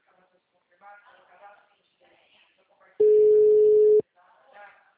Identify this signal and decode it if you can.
Ringback tone (Europe, 425 Hz), with speech.
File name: out- -2013- - - .31224